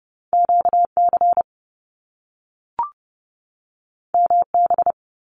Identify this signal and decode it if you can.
Morse code, heard as a keyed tone, about 30 wpm, 700 Hz. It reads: QC MB
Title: Canadian Provinces and Territories 30wpm